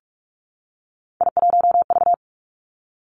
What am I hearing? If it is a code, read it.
Morse code, heard as a keyed tone, about 46 wpm, 700 Hz. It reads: I1V
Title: Three Challenging Letter-Number Combinations 45wpm